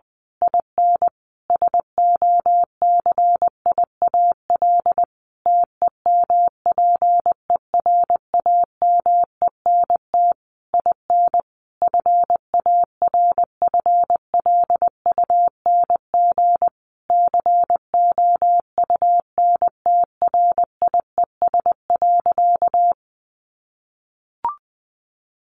Morse code, heard as a keyed tone, about 20 wpm, 700 Hz. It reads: IN SOCIAL TEMPERAMENT IN FARFLUNG COUNTRIES.
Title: A Sound Of Thunder 20wpm